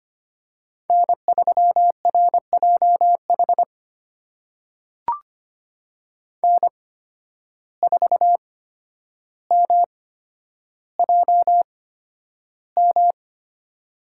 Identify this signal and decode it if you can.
Morse code, heard as a keyed tone, about 25 wpm, spaced as for 11 wpm, 700 Hz. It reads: N3RJH N4MJM
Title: USA Call Signs - 5 Characters - 8x Character Spacing - 25wpm